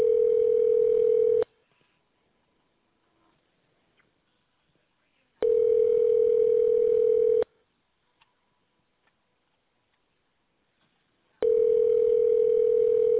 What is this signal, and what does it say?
Ringback tone (North America), 440 + 480 Hz.